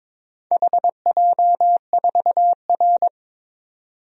Morse code, heard as a keyed tone, about 22 wpm, 700 Hz. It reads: HJ4R
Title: DXCC-100 Call Signs - 4 Characters - 22wpm